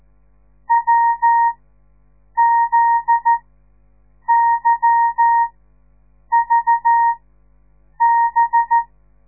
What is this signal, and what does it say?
Morse code, heard as a keyed tone, about 14 wpm, spaced as for 7 wpm, 930 Hz. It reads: WZYVB
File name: radiomorningstar / morse / les 9